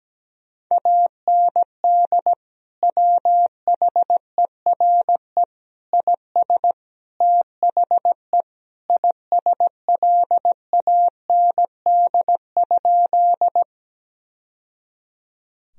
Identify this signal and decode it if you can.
Morse code, heard as a keyed tone, about 17 wpm, 700 Hz. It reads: AND WHERE IS THE ISLAND?
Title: Sentences from Top 1000 Words - Review Part 3 - 17wpm